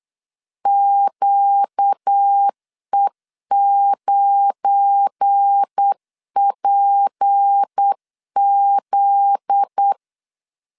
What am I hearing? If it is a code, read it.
Morse code, heard as a keyed tone, about 8 wpm, 800 Hz. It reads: QE9PZ
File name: LEZIONI / 8 YJ45 / 8 YJ45 8wpm 3214